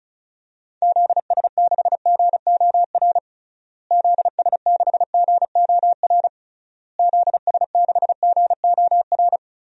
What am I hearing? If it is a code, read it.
Morse code, heard as a keyed tone, about 36 wpm, 700 Hz. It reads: ZS6GOR ZS6GOR ZS6GOR
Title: DXCC-100 Call Signs - 6 Characters - Repeated 3x - 35wpm